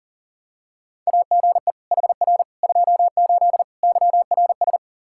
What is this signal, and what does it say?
Morse code, heard as a keyed tone, about 40 wpm, 700 Hz. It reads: AGE HR 28 YRS
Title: Practice QSO - Exchange 3 - More Info - 40wpm